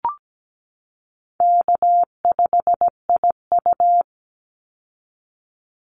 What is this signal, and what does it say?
Morse code, heard as a keyed tone, about 17 wpm, 700 Hz. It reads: K5IU
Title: Easy Call Signs Repeated 2x - 17wpm